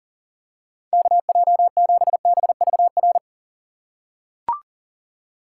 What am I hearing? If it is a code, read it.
Morse code, heard as a keyed tone, about 40 wpm, 700 Hz. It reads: KJ7BVR
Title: USA Call Signs - 6 Characters - 2x Character Spacing - 40wpm